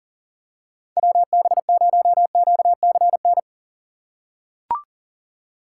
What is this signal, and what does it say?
Morse code, heard as a keyed tone, about 37 wpm, 700 Hz. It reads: WB0QCN
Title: USA Call Signs - 6 Characters - 4x Character Spacing - 40wpm